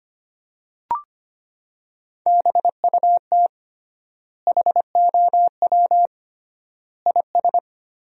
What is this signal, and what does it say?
Morse code, heard as a keyed tone, about 25 wpm, 700 Hz. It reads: BUT HOW IS
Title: 3x Word-Spacing - Sentences from Top 100 Words - 25wpm